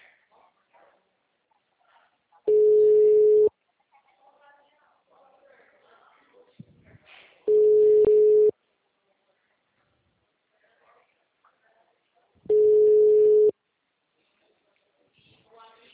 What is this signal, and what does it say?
Ringback tone (Europe, 425 Hz), with speech.